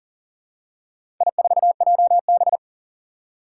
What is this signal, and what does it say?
Morse code, heard as a keyed tone, about 41 wpm, 700 Hz. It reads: I4JB
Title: Four Challenging Letter-Number Combinations 40wpm